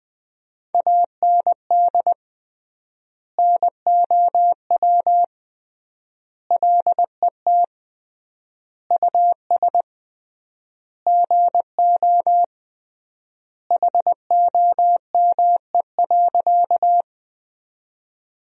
Morse code, heard as a keyed tone, about 20 wpm, 700 Hz. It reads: AND NOW LET US GO HOME.
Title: 3x Word-Spacing - Sentences from Top 300 Words - 20wpm